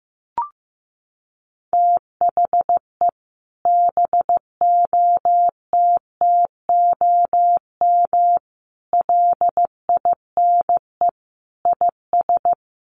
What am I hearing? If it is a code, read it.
Morse code, heard as a keyed tone, about 15 wpm, 700 Hz. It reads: THE BOTTOM LINE IS
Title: Sets of 5 Words Encore 15wpm